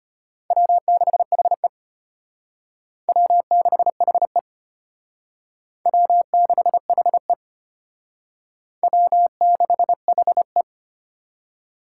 Morse code, heard as a keyed tone, about 32 wpm, 700 Hz. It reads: W6HE W6HE W6HE W6HE